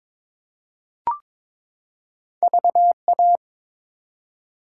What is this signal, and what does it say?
Morse code, heard as a keyed tone, about 22 wpm, 700 Hz. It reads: VA
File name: ARRL Sections v2 - 22wpm